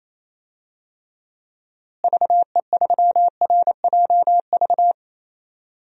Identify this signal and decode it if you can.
Morse code, heard as a keyed tone, about 28 wpm, 700 Hz. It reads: VE3RJV
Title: DXCC-100 Call Signs - 6 Characters - Repeated 2x - 28wpm